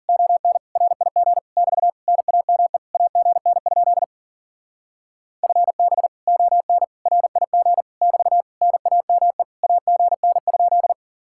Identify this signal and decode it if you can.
Morse code, heard as a keyed tone, about 50 wpm, 700 Hz. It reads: ON RIG <BT> NAME AGN? FB ON RIG <BT> NAME AGN?